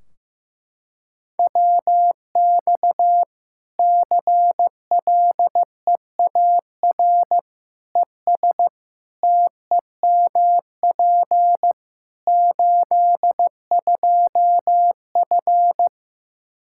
Morse code, heard as a keyed tone, about 15 wpm, 700 Hz. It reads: WX CLEAR ES TEMP 82F
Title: Practice QSO - Exchange 2 - Info - 15wpm